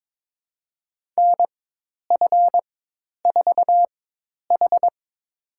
Morse code, heard as a keyed tone, about 22 wpm, spaced as for 10 wpm, 700 Hz. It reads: NF4H
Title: USA Call Signs - 4 Characters - 4x Character Spacing - 22wpm